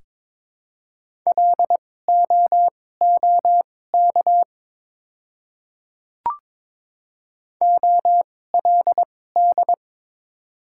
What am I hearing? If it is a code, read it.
Morse code, heard as a keyed tone, about 22 wpm, spaced as for 16 wpm, 700 Hz. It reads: LOOK OLD
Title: 2x Character Spacing - Top 200 Words - 22wpm